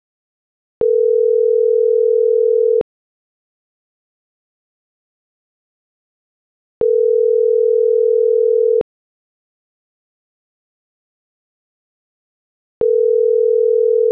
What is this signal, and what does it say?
Ringback tone (North America), 440 + 480 Hz.